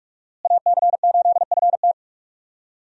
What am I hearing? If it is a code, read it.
Morse code, heard as a keyed tone, about 46 wpm, 700 Hz. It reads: AC8FT
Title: USA Call Signs - 5 Characters - Repeated 3x - 45wpm